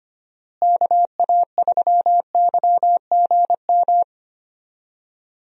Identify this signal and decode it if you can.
Morse code, heard as a keyed tone, about 25 wpm, 700 Hz. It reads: KA3YGM
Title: USA Call Signs - 6 Characters - Repeated 3x - 25wpm